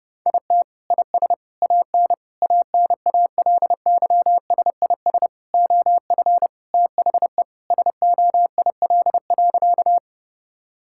Morse code, heard as a keyed tone, about 30 wpm, 700 Hz. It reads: IT IS AN ANALYSIS OF THE SOIL.
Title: Sentences from Top 1000 Words - Review Part 5 - 30wpm